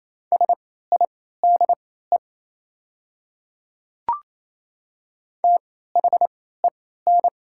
Morse code, heard as a keyed tone, about 28 wpm, spaced as for 16 wpm, 700 Hz. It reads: SIDE THEN
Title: 3x Character Spacing - Top 100 Words - 28wpm